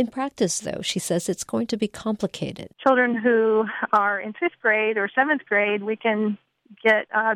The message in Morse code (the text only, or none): none